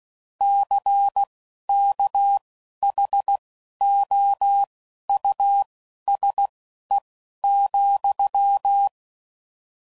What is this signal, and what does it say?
Morse code, heard as a keyed tone, about 16 wpm, spaced as for 11 wpm, 800 Hz. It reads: CKHOUSE,
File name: Suffixes / General / Houses